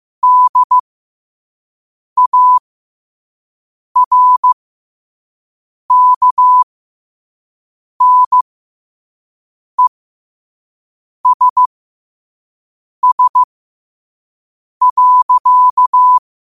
Morse code, heard as a keyed tone, about 15 wpm, spaced as for 5 wpm, 1000 Hz. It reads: DARKNESS.
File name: qotdfiles / 5wpm